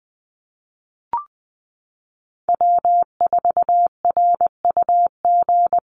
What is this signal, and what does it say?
Morse code, heard as a keyed tone, about 20 wpm, 700 Hz. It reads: W4RUG
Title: USA Call Signs Repeated 2x 20wpm